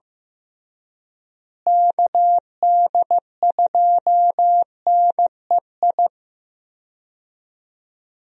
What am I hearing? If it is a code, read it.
Morse code, heard as a keyed tone, about 15 wpm, 700 Hz. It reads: KD2NEI
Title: USA Call Signs - 6 Characters - Repeated 2x - 15wpm